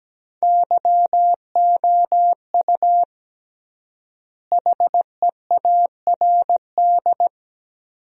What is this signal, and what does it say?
Morse code, heard as a keyed tone, about 17 wpm, 700 Hz. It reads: YOU HEARD
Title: 3x Word-Spacing - Sentences from Top 500 Words - 17wpm